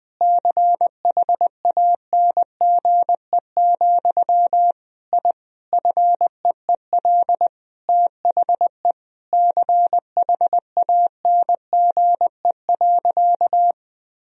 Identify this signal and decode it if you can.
Morse code, heard as a keyed tone, about 20 wpm, 700 Hz. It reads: CHANGE, I FEEL THE CHANGE.
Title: Sentences from Top 1000 Words - Review Part 1 - 20wpm